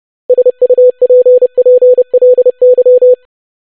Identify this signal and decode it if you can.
Morse code, heard as a keyed tone, about 30 wpm, 500 Hz. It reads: SUPPLY
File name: basic500-25wpm